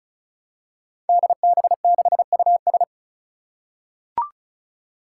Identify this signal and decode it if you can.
Morse code, heard as a keyed tone, about 36 wpm, 700 Hz. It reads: DB6US
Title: Five Challenging Letter-Number Combinations 35wpm